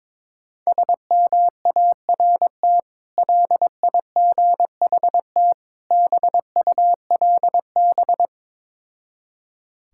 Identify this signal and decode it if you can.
Morse code, heard as a keyed tone, about 22 wpm, 690 Hz. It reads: SMART LIGHT BULB
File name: Difficult Words with Hints - 22wpm